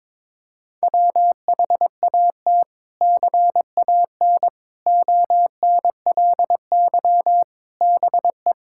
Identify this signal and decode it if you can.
Morse code, heard as a keyed tone, about 21 wpm, 700 Hz. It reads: WHAT CAN ONLY BE
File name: practice / 22wpm / Sets of 7 Words - 22wpm